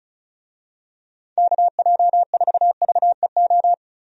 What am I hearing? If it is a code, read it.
Morse code, heard as a keyed tone, about 34 wpm, 700 Hz. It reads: KJ4VEO
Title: USA Call Signs - 6 Characters - 35wpm